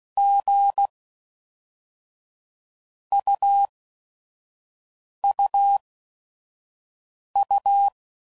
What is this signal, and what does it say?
Morse code, heard as a keyed tone, about 16 wpm, 800 Hz. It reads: G U U U
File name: Basic Letter Groups By Blocks